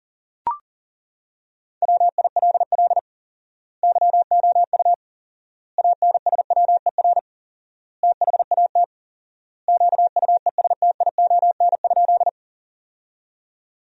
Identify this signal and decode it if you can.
Morse code, heard as a keyed tone, about 40 wpm, 700 Hz. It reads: WILL YOU ANSWER THAT QUESTION?
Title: 4x Word-Spacing - Sentences from Top 400 Words - 40wpm